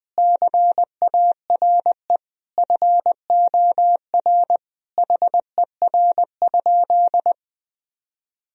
Morse code, heard as a keyed tone, about 20 wpm, 700 Hz. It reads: CARE FOR HER?
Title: Sentences from Top 1000 Words - Review Part 1 - 20wpm